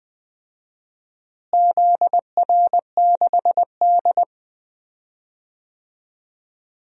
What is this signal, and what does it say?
Morse code, heard as a keyed tone, about 20 wpm, 700 Hz. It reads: ZR6D
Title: DXCC-100 Call Signs - 4 Characters - Repeated 2x - 20wpm